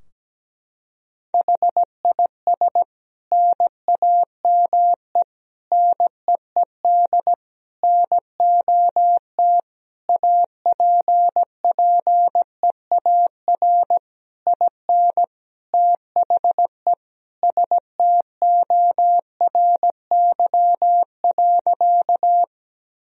Morse code, heard as a keyed tone, about 17 wpm, 700 Hz. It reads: HIS NAME NEED NOT APPEAR IN THE STORY.
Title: Sentences from Top 1000 Words - Review Part 2 - 17wpm